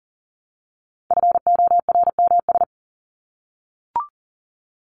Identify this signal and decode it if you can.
Morse code, heard as a keyed tone, about 40 wpm, 700 Hz. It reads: FORMS
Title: ES:S Suffix - Top 100 Words with Base Words 40wpm